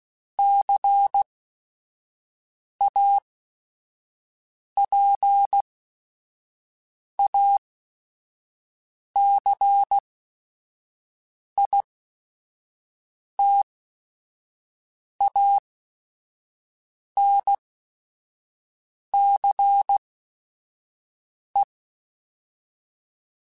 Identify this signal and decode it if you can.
Morse code, heard as a keyed tone, about 16 wpm, spaced as for 5 wpm, 800 Hz. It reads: CAPACITANCE
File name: Technical Subjects / Technical Subjects-mp3s / Reactance @5